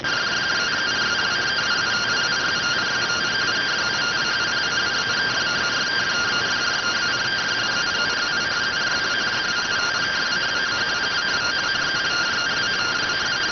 Two-tone teleprinter signal, about 170 Hz shift.